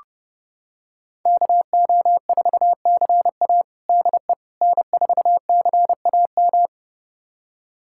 Morse code, heard as a keyed tone, about 30 wpm, 700 Hz. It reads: KO4CA DE N4CAM
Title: USA Callsign DE Callsign - 5 Characters - 30wpm